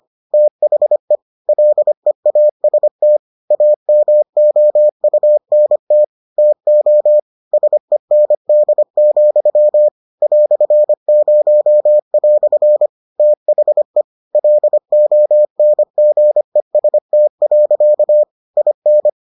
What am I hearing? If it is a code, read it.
Morse code, heard as a keyed tone, about 25 wpm, 600 Hz. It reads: THE LEAST AMOUNT TO SEND, "0" THE LONGEST. IN